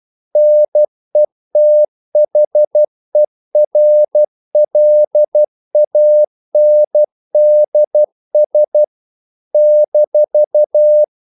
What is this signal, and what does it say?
Morse code, heard as a keyed tone, about 12 wpm, 600 Hz. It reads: NETHERLANDS -
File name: CW / Practice / NationalCapitals12